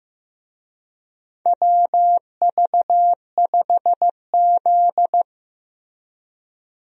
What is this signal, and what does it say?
Morse code, heard as a keyed tone, about 15 wpm, 700 Hz. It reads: WV5Z
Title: USA Call Signs - 4 Characters - Repeated 3x - 15wpm